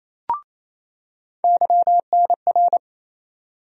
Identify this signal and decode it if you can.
Morse code, heard as a keyed tone, about 28 wpm, 700 Hz. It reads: YNR
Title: USA Call Sign Postfixes - 3 Letters 28wpm